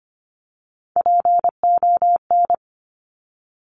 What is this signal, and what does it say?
Morse code, heard as a keyed tone, about 25 wpm, 700 Hz. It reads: PON
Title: Top Three Letter Combinations in Words 25wpm